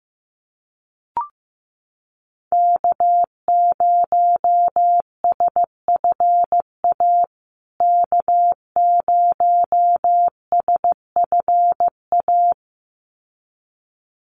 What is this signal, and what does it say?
Morse code, heard as a keyed tone, about 15 wpm, 700 Hz. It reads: K0SFA K0SFA
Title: USA Call Signs Repeated 2x 15wpm